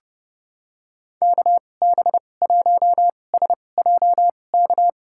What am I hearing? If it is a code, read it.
Morse code, heard as a keyed tone, about 31 wpm, spaced as for 22 wpm, 700 Hz. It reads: KB1SJK
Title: USA Call Signs - 6 Characters - 2x Character Spacing - 30wpm